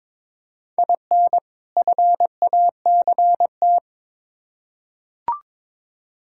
Morse code, heard as a keyed tone, about 22 wpm, 700 Hz. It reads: IN FACT
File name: Build A Phrase - 2-Word Phrases - 22wpm